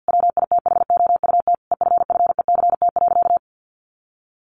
Morse code, heard as a keyed tone, about 50 wpm, 700 Hz. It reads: WITHOUT EFFECT.
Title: Sentences from Top 1400 Words 50wpm